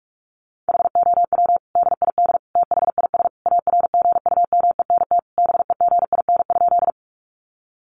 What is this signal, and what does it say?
Morse code, heard as a keyed tone, about 44 wpm, 700 Hz. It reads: HOW DID THIS ARGUMENT BEGIN?
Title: Sentences from Top 1400 Words 45wpm